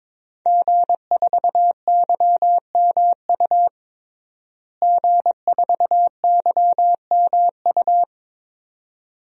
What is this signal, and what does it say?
Morse code, heard as a keyed tone, about 22 wpm, 700 Hz. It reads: G4YMU G4YMU
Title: DXCC-100 Call Signs - 5 Characters - Repeated 2x - 22wpm